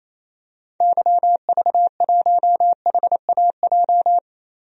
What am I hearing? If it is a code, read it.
Morse code, heard as a keyed tone, about 28 wpm, 700 Hz. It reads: YV1HAJ
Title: DXCC-100 Call Signs - 6 Characters - Repeated 2x - 28wpm